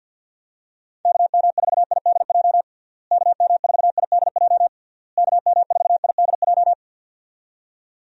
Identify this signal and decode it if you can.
Morse code, heard as a keyed tone, about 50 wpm, 700 Hz. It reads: KM4IDJ KM4IDJ KM4IDJ
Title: USA Call Signs - 6 Characters - Repeated 3x - 50wpm